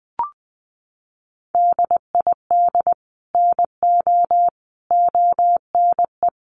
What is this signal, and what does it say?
Morse code, heard as a keyed tone, about 20 wpm, 700 Hz. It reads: DID NO ONE
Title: Sentences from Top 1100 Words 20wpm